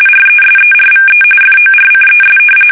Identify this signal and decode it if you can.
RTTY, about 45.45 baud. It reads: RESERVE FORCES
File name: fsk850 / usnavy5